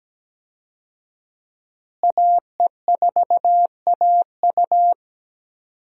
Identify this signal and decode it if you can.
Morse code, heard as a keyed tone, about 17 wpm, 700 Hz. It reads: AE4AU
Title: USA Call Signs - 5 Characters - Repeated 2x - 17wpm